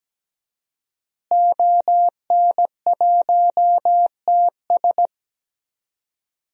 Morse code, heard as a keyed tone, about 17 wpm, 700 Hz. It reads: ON1TS
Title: DXCC-100 Call Signs - 5 Characters - Repeated 2x - 17wpm